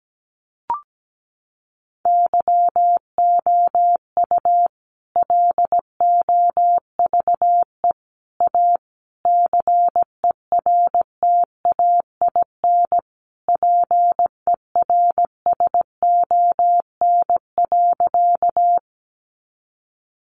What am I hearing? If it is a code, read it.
Morse code, heard as a keyed tone, about 17 wpm, 700 Hz. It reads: YOU LOVE A CERTAIN PERSON.
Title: Sentences from Top 500 Words 17wpm